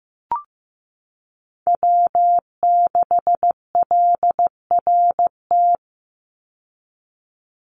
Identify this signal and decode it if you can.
Morse code, heard as a keyed tone, about 15 wpm, 700 Hz. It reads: W6LRT
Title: USA Call Signs 15wpm